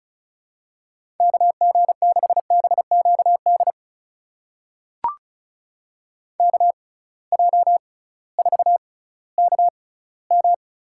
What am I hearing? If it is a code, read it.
Morse code, heard as a keyed tone, about 36 wpm, spaced as for 19 wpm, 700 Hz. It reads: KG6BQD KJ4KM
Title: USA Call Signs - 6 Characters - 6x Character Spacing - 35wpm